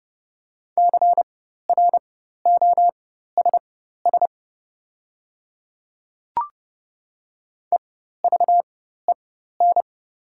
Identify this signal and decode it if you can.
Morse code, heard as a keyed tone, about 30 wpm, spaced as for 14 wpm, 700 Hz. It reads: CROSS EVEN
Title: 4x Character Spacing - Top 300 Words - 30wpm